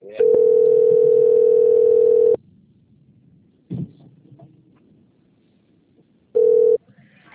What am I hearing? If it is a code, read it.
Ringback tone (North America), 440 + 480 Hz.
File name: out- - - - - .4805